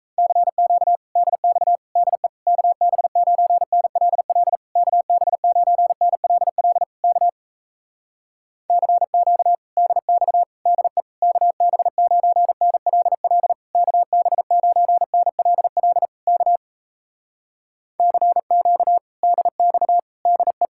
Morse code, heard as a keyed tone, about 37 wpm, 700 Hz. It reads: CQ DX DE KB9NLL KB9NLL K CQ DX DE KB9NLL KB9NLL K CQ DX DE